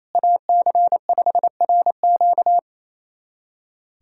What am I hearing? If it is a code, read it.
Morse code, heard as a keyed tone, about 28 wpm, 700 Hz. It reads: AC5RQ
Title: USA Callsign DE Callsign - 5 Characters - 28wpm